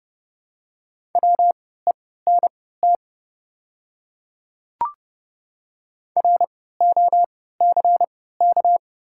Morse code, heard as a keyed tone, about 31 wpm, spaced as for 17 wpm, 700 Hz. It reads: WENT ROCK
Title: 3x Character Spacing - Top 500 Words - 30wpm